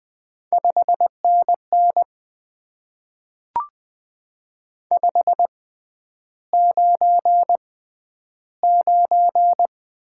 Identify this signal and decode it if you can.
Morse code, heard as a keyed tone, about 20 wpm, spaced as for 10 wpm, 700 Hz. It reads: 5NN 599
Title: Top 90 QSO Elements - 6x Character Spacing - 20wpm